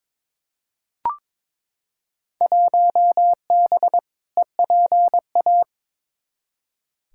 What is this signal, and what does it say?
Morse code, heard as a keyed tone, about 22 wpm, 700 Hz. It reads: 1B EPA
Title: Field Day Exchanges Encore - 22wpm